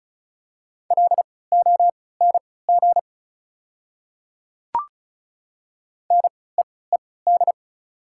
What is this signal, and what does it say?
Morse code, heard as a keyed tone, about 34 wpm, spaced as for 20 wpm, 690 Hz. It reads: LONG NEED
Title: 3x Character Spacing - Top 300 Words - 35wpm